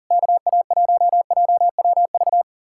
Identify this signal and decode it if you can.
Morse code, heard as a keyed tone, about 51 wpm, 700 Hz. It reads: KA1JWV